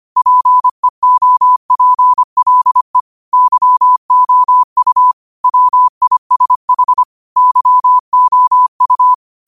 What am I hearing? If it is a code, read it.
Morse code, heard as a keyed tone, about 25 wpm, 1000 Hz. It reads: PEOPLE YOU WISH YOU
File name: qotdfiles / 25wpm